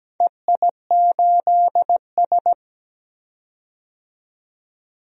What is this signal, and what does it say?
Morse code, heard as a keyed tone, about 17 wpm, 700 Hz. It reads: EI8S
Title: DXCC-100 Call Signs - 4 Characters - 17wpm